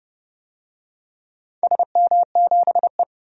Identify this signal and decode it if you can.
Morse code, heard as a keyed tone, about 30 wpm, 700 Hz. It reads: SM7E
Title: DXCC-100 Call Signs - 4 Characters - 30wpm